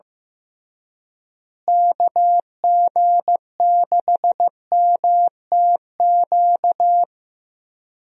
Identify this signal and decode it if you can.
Morse code, heard as a keyed tone, about 15 wpm, 700 Hz. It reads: KG6MTQ
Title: USA Call Signs - 6 Characters - Repeated 3x - 15wpm